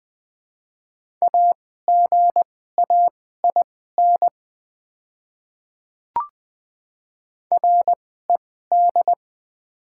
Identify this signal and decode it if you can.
Morse code, heard as a keyed tone, about 20 wpm, spaced as for 14 wpm, 700 Hz. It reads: AGAIN RED
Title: 2x Character Spacing - Top 500 Words - 20wpm